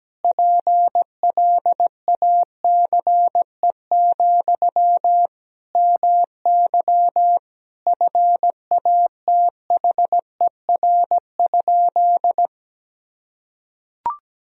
Morse code, heard as a keyed tone, about 17 wpm, 700 Hz. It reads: PLACE, MY FATHER?
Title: Sentences from Top 400 Words and Abbreviations 17wpm